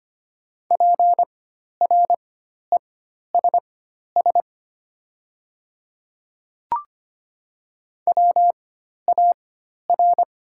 Morse code, heard as a keyed tone, about 25 wpm, spaced as for 12 wpm, 700 Hz. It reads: PRESS WAR
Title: 4x Character Spacing - Top 500 Words - 25wpm